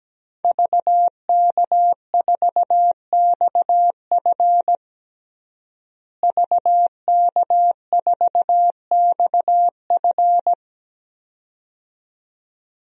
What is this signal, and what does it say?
Morse code, heard as a keyed tone, about 17 wpm, 700 Hz. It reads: VK4XF VK4XF
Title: DXCC-100 Call Signs - 5 Characters - Repeated 2x - 17wpm